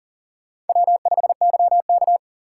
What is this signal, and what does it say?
Morse code, heard as a keyed tone, about 51 wpm, 700 Hz. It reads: W5YK